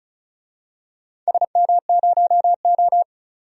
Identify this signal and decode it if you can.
Morse code, heard as a keyed tone, about 34 wpm, 700 Hz. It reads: SM0O
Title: DXCC-100 Call Signs - 4 Characters - 35wpm